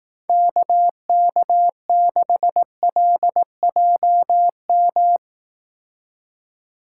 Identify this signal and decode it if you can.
Morse code, heard as a keyed tone, about 18 wpm, 700 Hz. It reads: KK6LJM
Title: Speed Racing - USA 2x3 Call Signs 15wpm